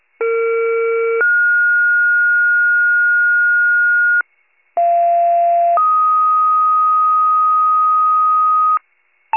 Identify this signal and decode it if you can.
Two-tone page, heard as a one-second tone, then a three-second tone, about 460 then 1430 Hz, about 690 then 1250 Hz, with noise. It is heard from 0.2 to 8.8 s.
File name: pager / 21 FIRE / wcalert